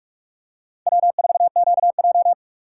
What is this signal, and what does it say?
Morse code, heard as a keyed tone, about 46 wpm, 700 Hz. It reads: W4QJ
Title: USA Call Signs - 4 Characters - 2x Character Spacing - 45wpm